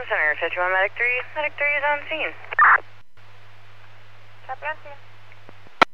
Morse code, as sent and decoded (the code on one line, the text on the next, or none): none